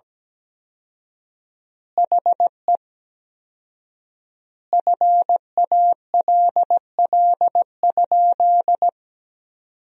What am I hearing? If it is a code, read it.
Morse code, heard as a keyed tone, about 17 wpm, 700 Hz. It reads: HE FALL?
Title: 4x Word-Spacing - Sentences from Top 500 Words - 17wpm